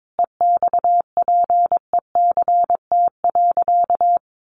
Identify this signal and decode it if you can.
Morse code, heard as a keyed tone, about 22 wpm, 700 Hz. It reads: EXPECT.
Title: Sentences from Top 1100 Words 22wpm